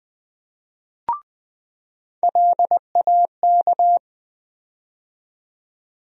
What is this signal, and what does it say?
Morse code, heard as a keyed tone, about 20 wpm, 700 Hz. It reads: LAK
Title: USA Call Sign Postfixes - 3 Letters 20wpm